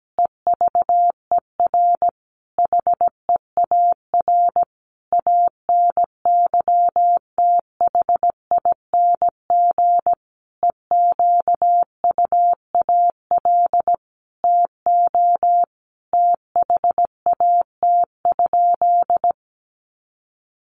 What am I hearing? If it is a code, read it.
Morse code, heard as a keyed tone, about 17 wpm, 700 Hz. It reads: EVER HEAR ANYTHING EQUAL TO THAT?
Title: Sentences from Top 1400 Words 17wpm